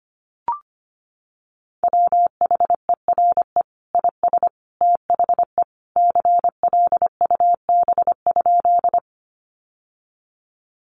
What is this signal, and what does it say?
Morse code, heard as a keyed tone, about 25 wpm, 700 Hz. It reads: WHERE IS THE CLUB?
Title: Sentences from Top 1200 Words 25wpm